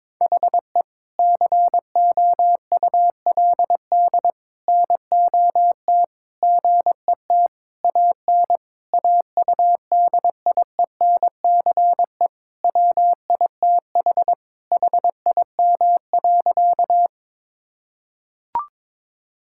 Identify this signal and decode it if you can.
Morse code, heard as a keyed tone, about 22 wpm, 700 Hz. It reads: HE COULD NOT GET AN AUDIENCE WITH HIM.
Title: Sentences from Top 1000 Words - Review Part 5 - 22wpm